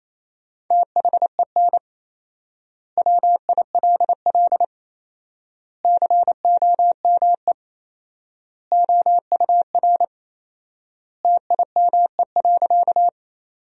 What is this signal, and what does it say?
Morse code, heard as a keyed tone, about 28 wpm, 700 Hz. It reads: THEN WILL COME OUR TIME.
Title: 4x Word-Spacing - Sentences from Top 200 Words - 28wpm